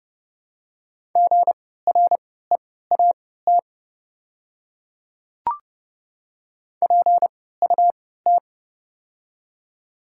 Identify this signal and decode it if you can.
Morse code, heard as a keyed tone, about 30 wpm, spaced as for 17 wpm, 690 Hz. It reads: GREAT PUT
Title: 3x Character Spacing - Top 200 Words - 30wpm